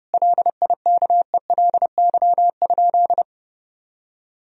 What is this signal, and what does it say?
Morse code, heard as a keyed tone, about 30 wpm, 700 Hz. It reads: LIKELY?
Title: Sentences from Top 1000 Words - Review Part 4 - 30wpm